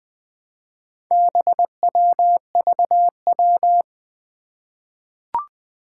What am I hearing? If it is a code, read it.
Morse code, heard as a keyed tone, about 20 wpm, 700 Hz. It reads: BWVW
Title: Four Challenging Letter-Number Combinations 20wpm